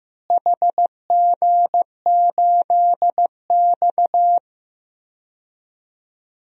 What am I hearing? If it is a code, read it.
Morse code, heard as a keyed tone, about 15 wpm, 700 Hz. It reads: HG8X
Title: DXCC-100 Call Signs - 4 Characters - Repeated 3x - 15wpm